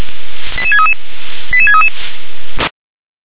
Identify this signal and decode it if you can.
Five-tone selective call (ZVEI), heard as quick runs of five tones, tones of about 70 ms. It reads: 88633 88633
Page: /357/SDR/ZVEI/88631/